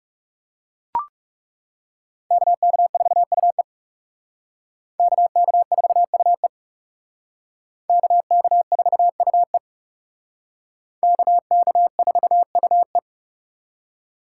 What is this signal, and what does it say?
Morse code, heard as a keyed tone, about 36 wpm, 700 Hz. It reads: KK4UE KK4UE KK4UE KK4UE
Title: Speed Racing - USA 2x2 Call Signs 30wpm